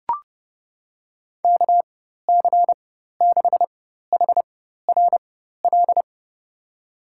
Morse code, heard as a keyed tone, about 30 wpm, spaced as for 14 wpm, 700 Hz. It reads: KC6HRL
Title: USA Call Signs - 4x Character Spacing - 30wpm